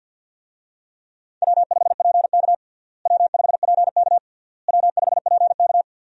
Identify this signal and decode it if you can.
Morse code, heard as a keyed tone, about 52 wpm, 700 Hz. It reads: W5PK W5PK W5PK
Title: USA Call Signs - 4 Characters - Repeated 3x - 50wpm